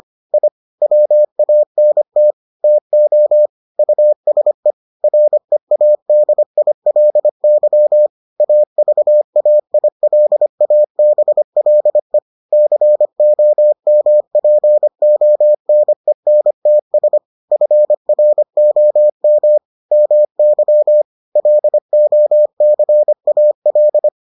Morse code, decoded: I WANT TO USE READILY AVAILABLE COMPONENTS FROM MY LOCAL

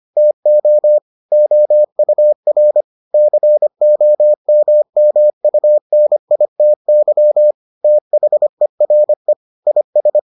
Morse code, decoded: TO OUR COMMUNITY THERE IS

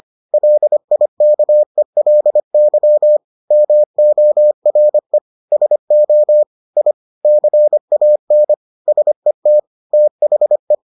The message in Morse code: LIKELY MORE SO I CAN SET THE

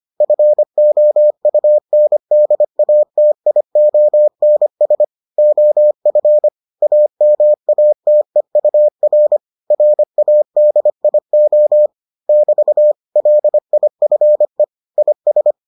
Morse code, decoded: FOUNDATIONS OF AMATEUR RADIO <BT> LIFE IS

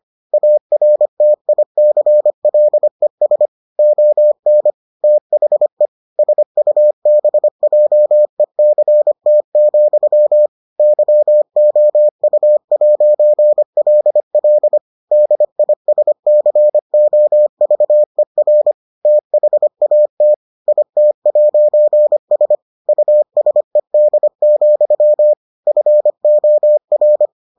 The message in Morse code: ARTICLES ON THE SUBJECT, YOU'LL DISCOVER THAT IT'S USED, FOR

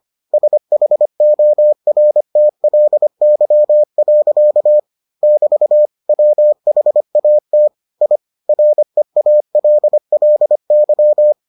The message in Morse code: SHORTLY. <BT> WHAT I REALLY

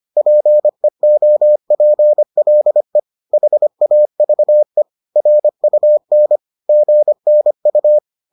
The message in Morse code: PEOPLE HAVE RUN GNU

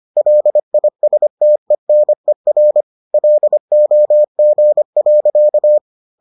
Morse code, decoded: LISTENER LOG.

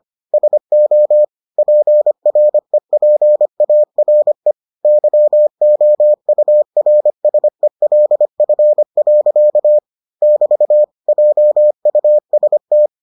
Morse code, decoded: SO PREPARE YOURSELF. <BT> JUST

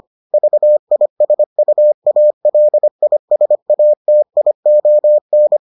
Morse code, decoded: VISUALISATION

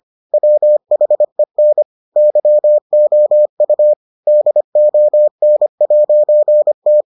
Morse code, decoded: WHEN YOU DON'T

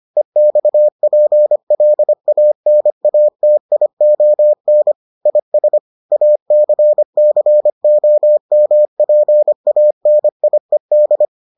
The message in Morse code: EXPLANATION IS ACCOMPANIED

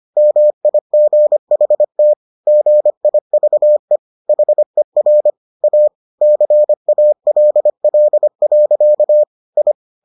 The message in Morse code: MIGHT GIVE HER A CALL. I

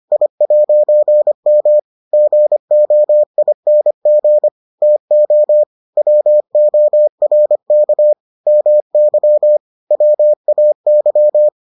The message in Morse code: I'M GOING TO WORK MY WAY

